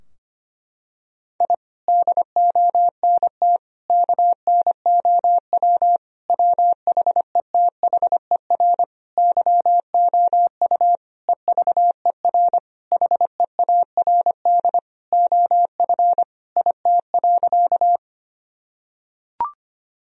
Morse code, heard as a keyed tone, about 25 wpm, 700 Hz. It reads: I DONT KNOW WHETHER YOU EVER HEARD OF IT.
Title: Sentences from Top 1000 Words - Review Part 3 - 25wpm